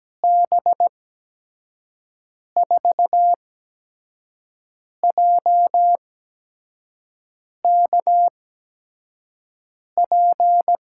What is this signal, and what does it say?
Morse code, heard as a keyed tone, about 17 wpm, spaced as for 5 wpm, 700 Hz. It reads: B4JKP
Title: USA Call Signs - 6 Characters - 8x Character Spacing - 17wpm